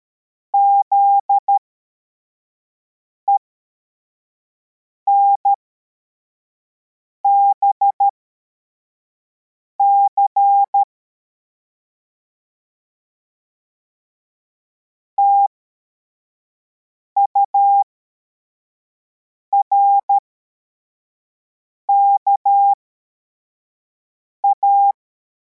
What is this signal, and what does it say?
Morse code, heard as a keyed tone, about 13 wpm, spaced as for 4 wpm, 800 Hz. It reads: ZENBC TURKA